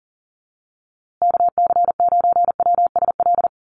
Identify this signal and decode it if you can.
Morse code, heard as a keyed tone, about 40 wpm, 700 Hz. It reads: KC9WSL
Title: USA Call Signs Repeated 2x 40wpm